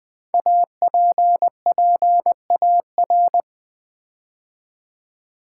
Morse code, heard as a keyed tone, about 20 wpm, 700 Hz. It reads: APPAR
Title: Build A Word - 5 Letters At A Time - 10-Letter Words - 20wpm